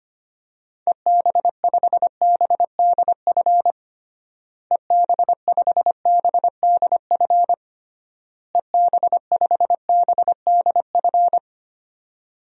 Morse code, heard as a keyed tone, about 25 wpm, 700 Hz. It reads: EB5BDF EB5BDF EB5BDF
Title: DXCC-100 Call Signs - 6 Characters - Repeated 3x - 25wpm